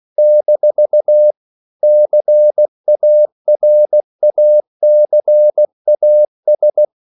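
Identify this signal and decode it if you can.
Morse code, heard as a keyed tone, about 16 wpm, 600 Hz. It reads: - CARACAS
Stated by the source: CW practice of national capitals at 16 words per minute, from ON6ZQ.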